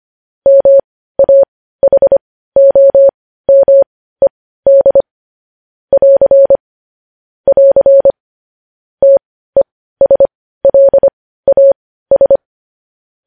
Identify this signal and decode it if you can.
Morse code, heard as a keyed tone, about 25 wpm, spaced as for 15 wpm, 550 Hz. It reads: MAHOMED <AR> <AR> TESLAS
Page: news headlines in morse code at 15 wpm